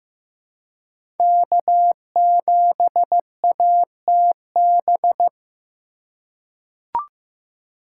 Morse code, heard as a keyed tone, about 15 wpm, 700 Hz. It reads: K7ATB
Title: USA Call Signs - 5 Characters - 2x Character Spacing - 15wpm